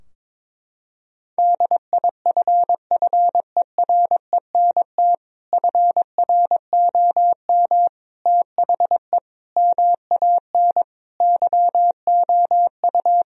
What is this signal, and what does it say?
Morse code, heard as a keyed tone, about 22 wpm, 700 Hz. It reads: DIFFERENT FROM THE MAN YOU
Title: Sentences from Top 1000 Words - Review Part 3 - 22wpm